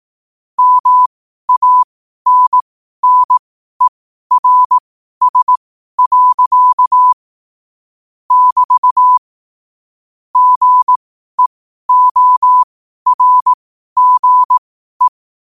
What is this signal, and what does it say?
Morse code, heard as a keyed tone, about 18 wpm, spaced as for 12 wpm, 1000 Hz. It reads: MANNERS. <BT> GEORGE
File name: qotdfiles / 12wpm